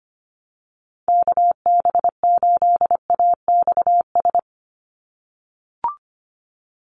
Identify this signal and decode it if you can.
Morse code, heard as a keyed tone, about 25 wpm, 700 Hz. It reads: KB8AXS
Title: USA Call Signs 25wpm